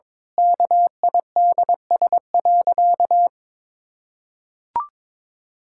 Morse code, heard as a keyed tone, about 22 wpm, 700 Hz. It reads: KIDS.
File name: Difficult Sentences with Hints - 22wpm